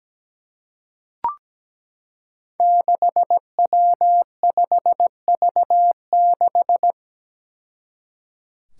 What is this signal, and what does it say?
Morse code, heard as a keyed tone, about 17 wpm, 700 Hz. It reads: 6W5V6
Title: Five Challenging Letter-Number Combinations 17wpm